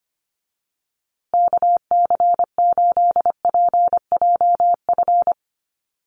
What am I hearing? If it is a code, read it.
Morse code, heard as a keyed tone, about 25 wpm, 700 Hz. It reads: KC8PJF
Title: USA Call Signs Repeated 3x 25wpm